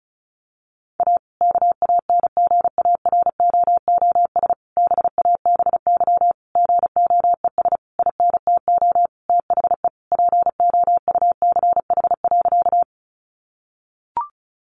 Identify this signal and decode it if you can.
Morse code, heard as a keyed tone, about 34 wpm, 700 Hz. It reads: A KANGAROOS BABY GOES INTO THE POUCH.
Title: Easy Phrases 35wpm